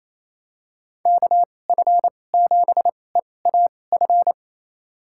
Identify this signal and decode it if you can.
Morse code, heard as a keyed tone, about 28 wpm, spaced as for 20 wpm, 690 Hz. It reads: KF7EAF
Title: USA Call Signs - 6 Characters - 2x Character Spacing - 28wpm